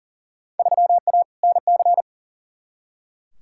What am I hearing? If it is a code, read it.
Morse code, heard as a keyed tone, about 39 wpm, 700 Hz. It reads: 3A NC
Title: Field Day Exchanges Encore - 40wpm